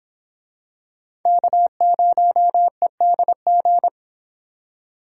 Morse code, heard as a keyed tone, about 23 wpm, 700 Hz. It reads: K0EDG